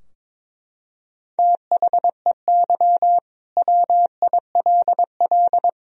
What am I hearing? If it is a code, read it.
Morse code, heard as a keyed tone, about 22 wpm, 700 Hz. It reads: THEY WILL
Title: Sentences from Top 1000 Words - Review Part 4 - 22wpm